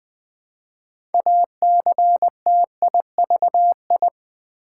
Morse code, heard as a keyed tone, about 20 wpm, 700 Hz. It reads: ACTIVI
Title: Build A Word - 3 Letters At A Time - 8-Letter Words - 20wpm